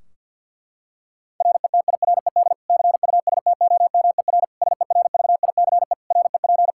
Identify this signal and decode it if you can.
Morse code, heard as a keyed tone, about 47 wpm, 700 Hz. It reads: RETIRED CUSTOMER SERVICE REP
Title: Practice QSO - Exchange 3 - More Info - 50wpm